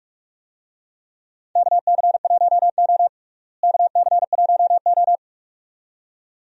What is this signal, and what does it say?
Morse code, heard as a keyed tone, about 46 wpm, 700 Hz. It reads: KC1O KC1O
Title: USA Call Signs - 4 Characters - Repeated 2x - 45wpm